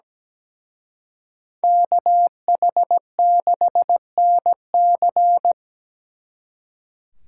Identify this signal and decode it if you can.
Morse code, heard as a keyed tone, about 20 wpm, 700 Hz. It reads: KH6NC
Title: Speed Racing - USA 2x2 Call Signs 17wpm